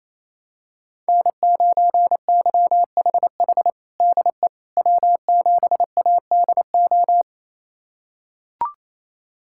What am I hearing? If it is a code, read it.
Morse code, heard as a keyed tone, about 28 wpm, 700 Hz. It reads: N9YHH DE W7ADO
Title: USA Callsign DE Callsign - 5 Characters - 28wpm